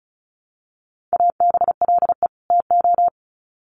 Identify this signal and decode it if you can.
Morse code, heard as a keyed tone, about 34 wpm, 700 Hz. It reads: ABLE TO
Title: Sets of 2 Words 35wpm